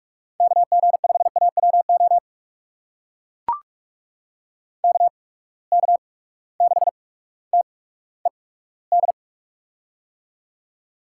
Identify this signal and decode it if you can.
Morse code, heard as a keyed tone, about 44 wpm, spaced as for 19 wpm, 700 Hz. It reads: KG5AWO KK6TED
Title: USA Call Signs - 6 Characters - 8x Character Spacing - 45wpm